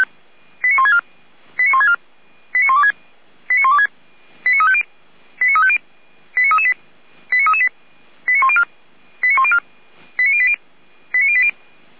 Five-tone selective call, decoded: 89164 89164 89126 89126 89350 89350 89338 89338 89104 89104 89080 89080

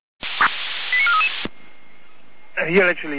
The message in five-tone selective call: 89433

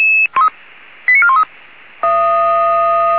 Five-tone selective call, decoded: 89413

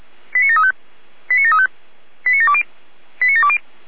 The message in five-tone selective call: 89735 89735 89720 89720

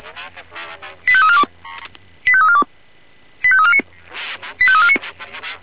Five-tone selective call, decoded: 85342 85342 85349 85349

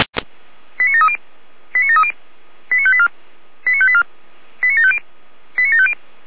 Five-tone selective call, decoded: 89730 89730 89584 89584 89750 89750

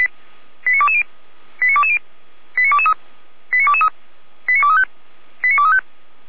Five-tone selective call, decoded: 89220 89220 89223 89223 89235 89235

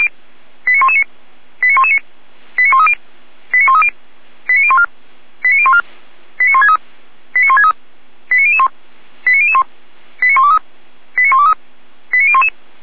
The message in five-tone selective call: 89119 89119 89130 89130 89014 89014 89163 89163 89001 89001 89123 89123 89011

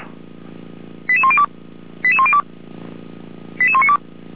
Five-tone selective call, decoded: 88102 88102 88192

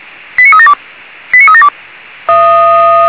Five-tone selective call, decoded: 89372 89372

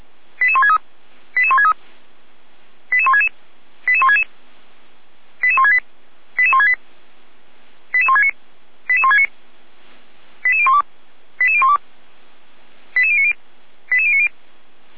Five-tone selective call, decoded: 88163 88163 88166 88166 88168 88168 88169 88169 88012 88012 88090 88090

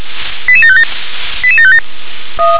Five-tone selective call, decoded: 88757 88757

